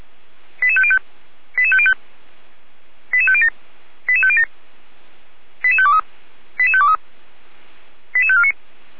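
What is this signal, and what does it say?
Five-tone selective call (ZVEI), heard as quick runs of five tones, tones of about 70 ms. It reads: 88505 88505 88507 88507 88523 88523 88549
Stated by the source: ZVEI five-tone selective call, code 88405, received on an SDR.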